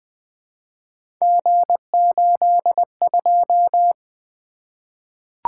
Morse code, heard as a keyed tone, about 20 wpm, 700 Hz. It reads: G82
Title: Three Letter-Number Combinations - 20wpm